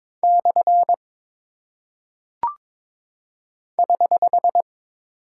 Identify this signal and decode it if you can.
Morse code, heard as a keyed tone, about 22 wpm, 700 Hz. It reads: / <HH>
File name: practice / 22wpm / Single Punctuation-Prosign - 22wpm